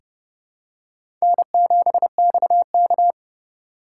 Morse code, heard as a keyed tone, about 30 wpm, 700 Hz. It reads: N7XK
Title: USA Call Signs - 3x Character Spacing - 30wpm